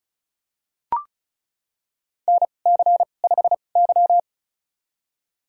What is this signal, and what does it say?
Morse code, heard as a keyed tone, about 36 wpm, spaced as for 26 wpm, 700 Hz. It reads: NC5Y
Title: USA Call Signs - 4 Characters - 2x Character Spacing - 35wpm